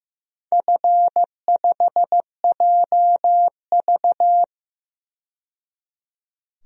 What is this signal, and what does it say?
Morse code, heard as a keyed tone, about 15 wpm, 700 Hz. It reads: F5JV
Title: DXCC-100 Call Signs - 4 Characters - 15wpm